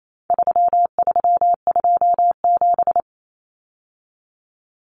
Morse code, heard as a keyed tone, about 28 wpm, 700 Hz. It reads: 3327
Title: Common Cut Numbers - 4-Digit 28wpm